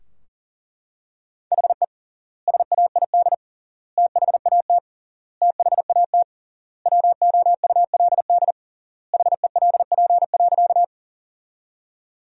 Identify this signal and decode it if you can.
Morse code, heard as a keyed tone, about 37 wpm, 700 Hz. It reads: HE SAID THAT THAT WOULD HELP.
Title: 3x Word-Spacing - Sentences from Top 200 Words - 40wpm